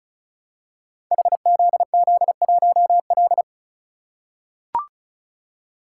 Morse code, heard as a keyed tone, about 34 wpm, 700 Hz. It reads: HZZ1L
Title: Code Groups - 5 Letter:Numbers - 35wpm